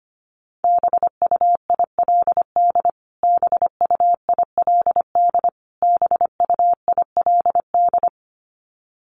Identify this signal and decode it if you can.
Morse code, heard as a keyed tone, about 25 wpm, 700 Hz. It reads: BUILD BUILD BUILD
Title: Top 500 Words Repeated 25wpm